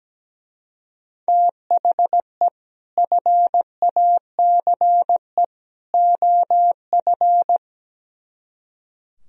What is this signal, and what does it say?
Morse code, heard as a keyed tone, about 17 wpm, 700 Hz. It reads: THE FACE OF
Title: Speed-Racing - Sets of 3 Words 17wpm